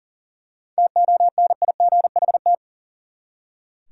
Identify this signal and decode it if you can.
Morse code, heard as a keyed tone, about 40 wpm, 700 Hz. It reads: TONIGHT
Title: Seven Letter Words - 40wpm